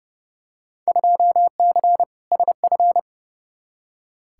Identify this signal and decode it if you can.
Morse code, heard as a keyed tone, about 30 wpm, 700 Hz. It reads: 2C SF
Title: Field Day Exchanges Encore - 30wpm